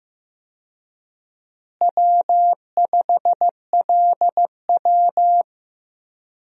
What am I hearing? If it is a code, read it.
Morse code, heard as a keyed tone, about 16 wpm, 700 Hz. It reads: W5LW
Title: USA Call Signs - 4 Characters - Repeated 3x - 15wpm